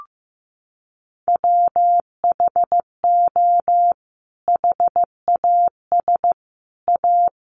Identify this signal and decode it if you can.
Morse code, heard as a keyed tone, about 15 wpm, 700 Hz. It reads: WHO HAS A
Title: Sentences from Top 1100 Words 15wpm